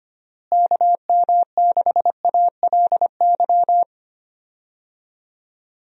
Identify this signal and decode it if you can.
Morse code, heard as a keyed tone, about 25 wpm, 700 Hz. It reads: KM6ALY
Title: USA Call Signs - 6 Characters - Repeated 2x - 25wpm